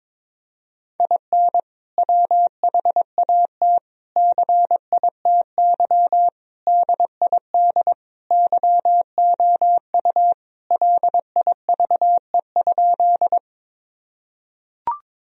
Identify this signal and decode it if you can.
Morse code, heard as a keyed tone, about 22 wpm, 700 Hz. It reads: IN WHAT CITY DID YOU LIVE?
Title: Sentences from Top 1000 Words - Review Part 1 - 22wpm